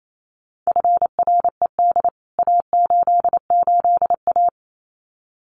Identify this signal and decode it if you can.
Morse code, heard as a keyed tone, about 28 wpm, 700 Hz. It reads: FRED A88A
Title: CWT Exchange 28wpm